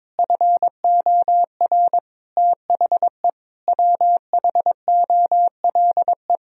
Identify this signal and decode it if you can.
Morse code, heard as a keyed tone, about 22 wpm, 700 Hz. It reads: FOR THE WHOLE
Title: Sentences from Top 1000 Words - Review Part 4 - 22wpm